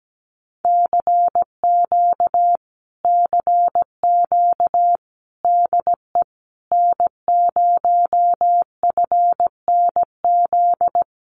Morse code, heard as a keyed tone, about 17 wpm, 700 Hz. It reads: CQ CQ DE N0FNZ
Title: Common QSO Phrases Encore 17wpm